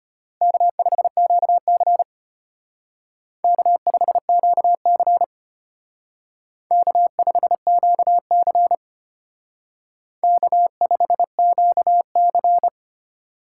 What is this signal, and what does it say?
Morse code, heard as a keyed tone, about 32 wpm, 700 Hz. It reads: K5QC K5QC K5QC K5QC